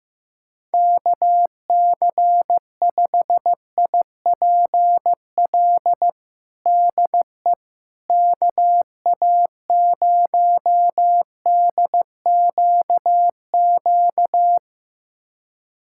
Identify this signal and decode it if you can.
Morse code, heard as a keyed tone, about 15 wpm, 700 Hz. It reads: KC5IPL DE KA0DQQ
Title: USA Callsign DE Callsign - 6 Characters - 15wpm